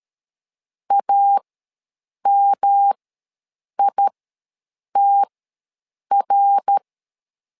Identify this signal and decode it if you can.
Morse code, heard as a keyed tone, about 12 wpm, spaced as for 7 wpm, 800 Hz. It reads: AMITR
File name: LEZIONI / 3 MWRO / 3 MWR ws 12wpm 1809